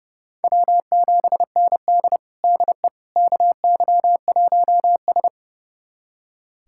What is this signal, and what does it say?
Morse code, heard as a keyed tone, about 31 wpm, 700 Hz. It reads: W7ND DE KY1S
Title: USA Callsign DE Callsign - 4 Characters - 30wpm